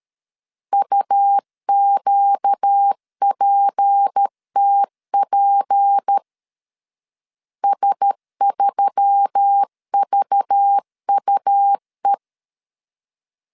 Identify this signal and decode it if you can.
Morse code, heard as a keyed tone, about 13 wpm, 800 Hz. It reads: UQPTP S3VUE
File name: LEZIONI / 6 PKX0 / 6 P 12wpm 2033